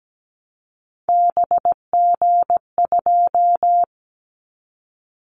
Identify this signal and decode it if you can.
Morse code, heard as a keyed tone, about 17 wpm, 700 Hz. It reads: BG2
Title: Callsign Prefixes for Top 100 DXCC - 17wpm